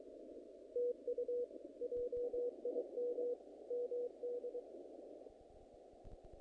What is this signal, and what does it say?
Morse code, heard as a keyed tone, about 24 wpm, 500 Hz. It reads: TU JIM MD
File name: contests / naqp-cw-k5tr-op-k5pi- -2000